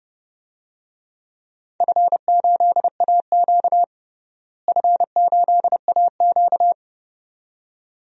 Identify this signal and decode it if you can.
Morse code, heard as a keyed tone, about 30 wpm, 700 Hz. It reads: F8AQ F8AQ
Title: DXCC-100 Call Signs - 4 Characters - Repeated 2x - 30wpm